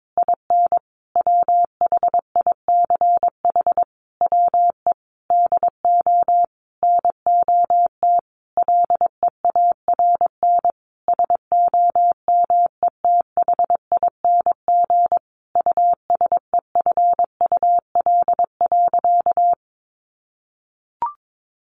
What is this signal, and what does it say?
Morse code, heard as a keyed tone, about 22 wpm, 700 Hz. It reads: IN WHICH WE DO NOT LEARN SOMETHING USEFUL.
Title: Sentences from Top 1400 Words 22wpm